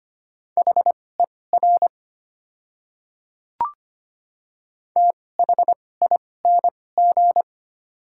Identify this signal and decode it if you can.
Morse code, heard as a keyed tone, about 25 wpm, spaced as for 18 wpm, 700 Hz. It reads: HER THING
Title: 2x Character Spacing - Top 100 Words - 25wpm